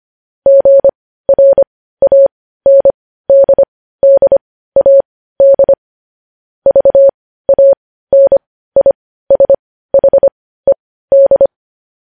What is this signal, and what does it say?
Morse code, heard as a keyed tone, about 25 wpm, spaced as for 15 wpm, 550 Hz. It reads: GRANDDAD VANISHED
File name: podcast / News headlines-1x-25x15